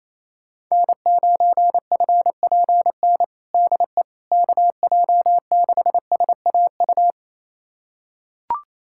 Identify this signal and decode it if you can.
Morse code, heard as a keyed tone, about 28 wpm, 700 Hz. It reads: N9FPN DE KJ6SAU
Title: USA Callsign DE Callsign 28wpm